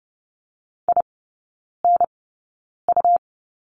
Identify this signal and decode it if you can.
Morse code, heard as a keyed tone, about 30 wpm, spaced as for 9 wpm, 700 Hz. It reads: INU
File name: Lesson 17e - Letter u - Three Characters 30wpm